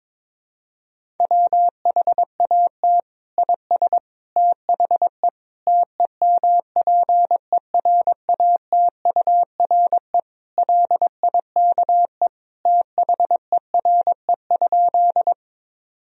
Morse code, heard as a keyed tone, about 22 wpm, 700 Hz. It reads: WHAT IS THE TEMPERATURE LIKE THERE?
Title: Sentences from Top 1000 Words - Review Part 4 - 22wpm